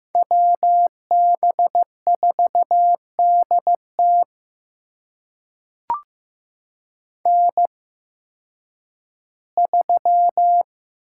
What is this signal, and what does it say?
Morse code, heard as a keyed tone, about 16 wpm, spaced as for 10 wpm, 700 Hz. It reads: WB4DT N3
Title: USA Call Signs - 8x Character Spacing - 15wpm